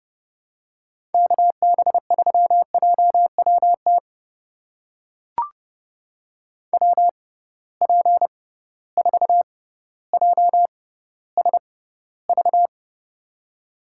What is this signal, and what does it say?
Morse code, heard as a keyed tone, about 30 wpm, spaced as for 15 wpm, 700 Hz. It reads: KB3JWT WP4JSV
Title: USA Call Signs - 6 Characters - 6x Character Spacing - 30wpm